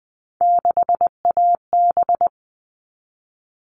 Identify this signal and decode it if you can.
Morse code, heard as a keyed tone, about 20 wpm, 700 Hz. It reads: 6AB
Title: All Cut Numbers - 3-Digit 20wpm